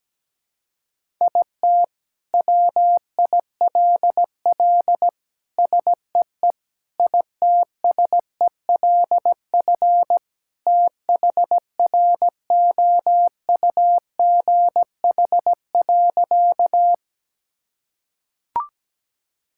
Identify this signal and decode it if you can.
Morse code, heard as a keyed tone, about 17 wpm, 700 Hz. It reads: IT WILL SEE ITSELF THROUGH.
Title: Sentences from Top 1000 Words - Review Part 5 - 17wpm